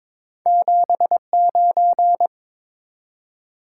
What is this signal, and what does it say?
Morse code, heard as a keyed tone, about 22 wpm, 700 Hz. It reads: 79
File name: practice / 22wpm / Two Letter-Number Combinations - 22wpm